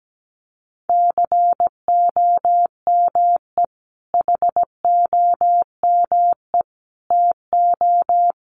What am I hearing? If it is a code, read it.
Morse code, heard as a keyed tone, about 17 wpm, 700 Hz. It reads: COME HOME TO
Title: Sentences from Top 1000 Words 17wpm